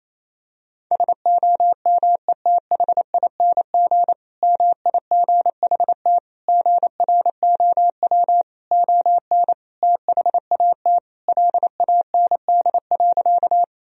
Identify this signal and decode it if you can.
Morse code, heard as a keyed tone, about 27 wpm, 700 Hz. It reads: SOMETHING MIGHT GROW ON THAT LAND.
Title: Sentences from Top 1000 Words - Review Part 1 - 28wpm